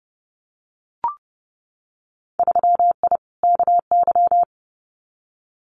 Morse code, heard as a keyed tone, about 30 wpm, 700 Hz. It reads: 3I KY
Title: Winter Field Day Exchanges 30wpm